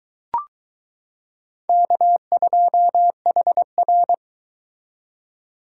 Morse code, heard as a keyed tone, about 22 wpm, 700 Hz. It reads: K2HR